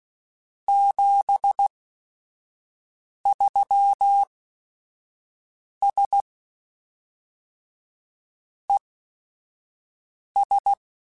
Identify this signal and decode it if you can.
Morse code, heard as a keyed tone, about 16 wpm, spaced as for 5 wpm, 800 Hz. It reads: 73S ES